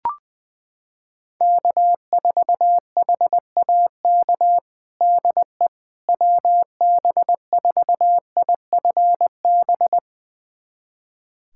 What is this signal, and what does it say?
Morse code, heard as a keyed tone, about 20 wpm, 700 Hz. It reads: K4HAK DE WB4IFB
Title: USA Callsign DE Callsign 20wpm